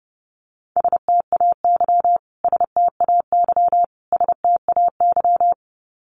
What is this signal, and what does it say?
Morse code, heard as a keyed tone, about 30 wpm, 700 Hz. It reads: STAY STAY STAY
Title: Top 500 Words Repeated 30wpm